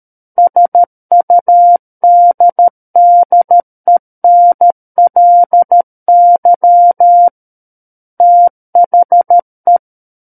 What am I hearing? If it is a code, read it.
Morse code, heard as a keyed tone, about 13 wpm, 700 Hz. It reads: SUDDENLY THE
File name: Book One / 04-Chapter 4 - 13 WPM